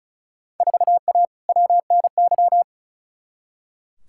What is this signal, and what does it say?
Morse code, heard as a keyed tone, about 34 wpm, 700 Hz. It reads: 4A WNY
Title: Field Day Exchanges Encore - 35wpm